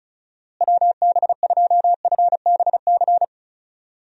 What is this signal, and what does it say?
Morse code, heard as a keyed tone, about 36 wpm, 700 Hz. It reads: WB2FBC
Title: USA Call Signs - 6 Characters - Repeated 3x - 35wpm